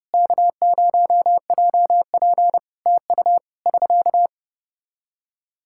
Morse code, heard as a keyed tone, about 30 wpm, 700 Hz. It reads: K0JP TU <SK>
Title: Practice QSO - Exchange 4 - Ending - 30wpm